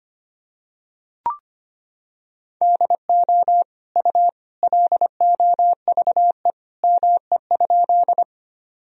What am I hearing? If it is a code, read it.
Morse code, heard as a keyed tone, about 25 wpm, 700 Hz. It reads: DO U LOVE ME?
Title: Sentences from Top 400 Words and Abbreviations 25wpm